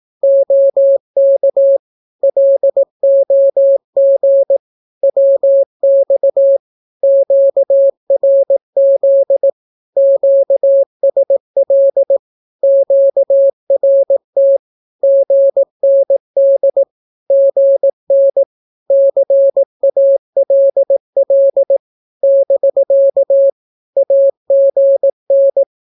Morse code, heard as a keyed tone, about 18 wpm, 550 Hz. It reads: OK LOG WX QRZ QSL QRT GND GN CALL <BK> AGN
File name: naukapodstaw / skroty / skroty